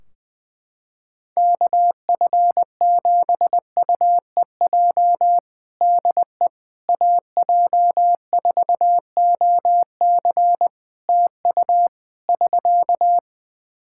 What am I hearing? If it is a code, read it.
Morse code, heard as a keyed tone, about 20 wpm, 700 Hz. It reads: KF7UEJ DE AJ4OC TU <SK>
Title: Practice QSO - Exchange 4 - Ending - 20wpm